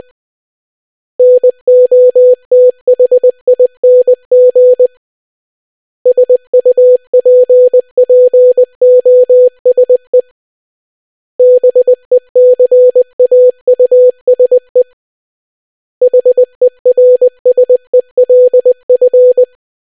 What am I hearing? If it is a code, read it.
Morse code, heard as a keyed tone, about 20 wpm, 500 Hz. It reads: NOTHING SUPPOSE BECAUSE HERSELF